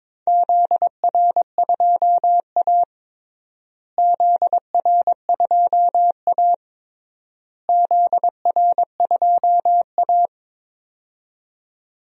Morse code, heard as a keyed tone, about 21 wpm, 700 Hz. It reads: ZR2A ZR2A ZR2A
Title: DXCC-100 Call Signs - 4 Characters - Repeated 3x - 22wpm